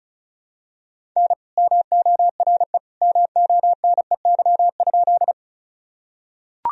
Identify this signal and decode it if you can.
Morse code, heard as a keyed tone, about 36 wpm, 700 Hz. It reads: N MORE MONEY?
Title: Sentences from Top 400 Words and Abbreviations 35wpm